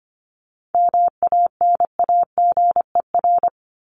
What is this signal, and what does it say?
Morse code, heard as a keyed tone, about 25 wpm, 700 Hz. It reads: MANAGER
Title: Top 1501-1750 Words 25wpm